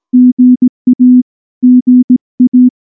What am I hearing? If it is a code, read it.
Morse code, heard as a keyed tone, about 19 wpm, 250 Hz. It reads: GA GA
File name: morse / morse-training-20WPM-4